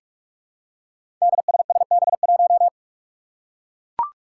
Morse code, heard as a keyed tone, about 44 wpm, 700 Hz. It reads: DSSB1
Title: Five Challenging Letter-Number Combinations 45wpm